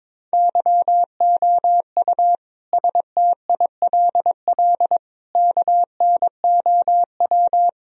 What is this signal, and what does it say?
Morse code, heard as a keyed tone, about 22 wpm, 700 Hz. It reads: YOU STILL KNOW
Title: Sentences from Top 1000 Words - Review Part 1 - 22wpm